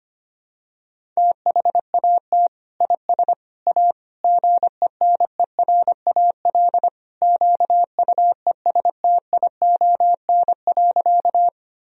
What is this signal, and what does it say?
Morse code, heard as a keyed tone, about 25 wpm, 700 Hz. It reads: THAT IS A GENERAL QUESTION.
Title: Sentences from Top 1000 Words - Review Part 3 - 25wpm